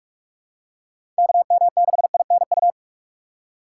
Morse code, heard as a keyed tone, about 46 wpm, 700 Hz. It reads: KM6INU
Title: USA Call Signs - 6 Characters - 4x Character Spacing - 45wpm